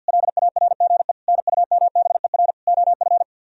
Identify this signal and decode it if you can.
Morse code, heard as a keyed tone, about 64 wpm, 700 Hz. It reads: LARGE NUMBER OF